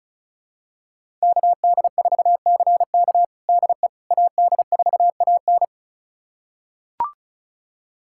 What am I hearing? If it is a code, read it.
Morse code, heard as a keyed tone, about 34 wpm, 700 Hz. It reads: KD4CK DE AD4AN
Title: USA Callsign DE Callsign - 5 Characters - 35wpm